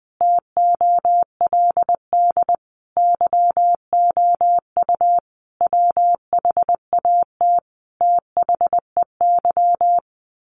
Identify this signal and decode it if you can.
Morse code, heard as a keyed tone, about 20 wpm, 700 Hz. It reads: TOLD YOU WHAT THEY
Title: Sentences from Top 1000 Words 20wpm